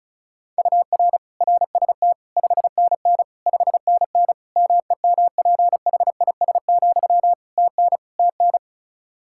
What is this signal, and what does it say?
Morse code, heard as a keyed tone, about 36 wpm, 700 Hz. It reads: UR RST 5NN 5NN MEMPHIS, TN TN
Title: Practice QSO - Exchange 1 - Beginning - 35wpm